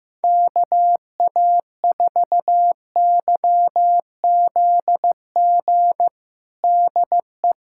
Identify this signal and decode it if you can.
Morse code, heard as a keyed tone, about 15 wpm, 700 Hz. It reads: KA4YZG DE
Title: USA Callsign DE Callsign - 6 Characters - 15wpm